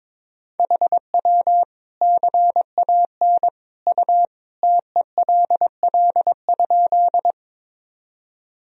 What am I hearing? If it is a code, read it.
Morse code, heard as a keyed tone, about 21 wpm, 700 Hz. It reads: HW CAN U TELL?
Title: Sentences from Top 200 Words and Abbreviations 22wpm